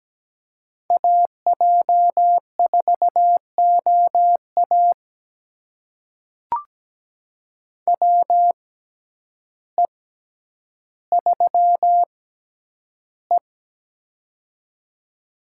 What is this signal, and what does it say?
Morse code, heard as a keyed tone, about 17 wpm, spaced as for 9 wpm, 700 Hz. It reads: AJ4OA WE3E
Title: USA Call Signs - 6x Character Spacing - 17wpm